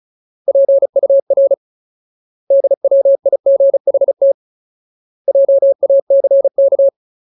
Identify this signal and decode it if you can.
Morse code, heard as a keyed tone, about 34 wpm, 550 Hz. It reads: PUR DWIGHT JACK
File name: Ultimate Word List / 35WPM Ultimate